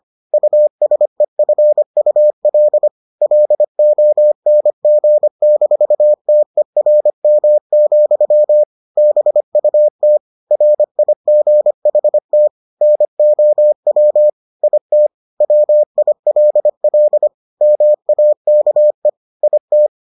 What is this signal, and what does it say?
Morse code, heard as a keyed tone, about 25 wpm, 600 Hz. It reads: USEFUL LONG-TERM, BUT RIGHT NOW IT WILL MAKE IT